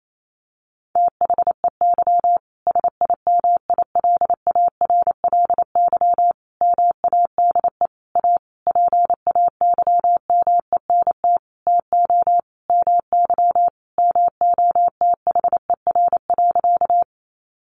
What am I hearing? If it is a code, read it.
Morse code, heard as a keyed tone, about 28 wpm, 700 Hz. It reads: THEY SIMILARLY MADE A PAYMENT TO MY MOTHER.
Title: Sentences from Top 1400 Words 28wpm